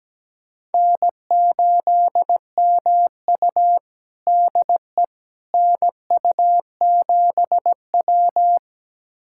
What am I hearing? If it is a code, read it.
Morse code, heard as a keyed tone, about 17 wpm, 700 Hz. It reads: N8MU DE NU7W
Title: USA Callsign DE Callsign - 4 Characters - 17wpm